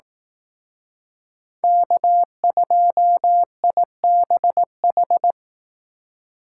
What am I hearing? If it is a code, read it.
Morse code, heard as a keyed tone, about 18 wpm, 700 Hz. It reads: K2IBH
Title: Speed Racing - USA 1x3 Call Signs 15wpm